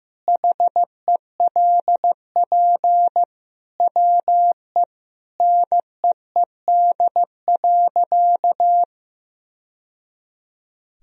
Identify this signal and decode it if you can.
Morse code, heard as a keyed tone, about 15 wpm, 700 Hz. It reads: HELP WE NEED.
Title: Sentences from Top 1000 Words - Review Part 1 - 15wpm